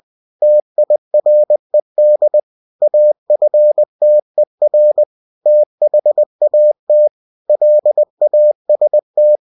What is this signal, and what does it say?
Morse code, heard as a keyed tone, about 20 wpm, 600 Hz. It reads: TIRED AFTER THAT LAST
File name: cw-books / winnie-the-pooh / winnie-20wpm-page-0101